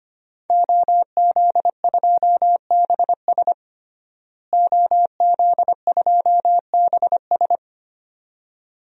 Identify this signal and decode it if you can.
Morse code, heard as a keyed tone, about 25 wpm, 700 Hz. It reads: OZ2BS OZ2BS
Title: DXCC-100 Call Signs - 5 Characters - Repeated 2x - 25wpm